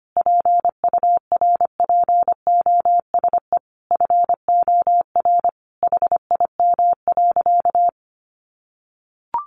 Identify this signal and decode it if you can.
Morse code, heard as a keyed tone, about 25 wpm, 700 Hz. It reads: PURPOSE FOR HIM.
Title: Sentences from Top 800 Words 25wpm